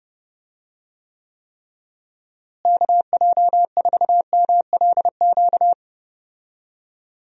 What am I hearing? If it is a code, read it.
Morse code, heard as a keyed tone, about 30 wpm, 700 Hz. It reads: KJ4MLQ
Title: USA Call Signs - 6 Characters - Repeated 3x - 30wpm